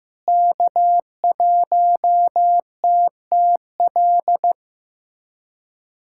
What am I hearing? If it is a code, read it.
Morse code, heard as a keyed tone, about 15 wpm, 700 Hz. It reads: K1TTL
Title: USA Callsign DE Callsign - 5 Characters - 15wpm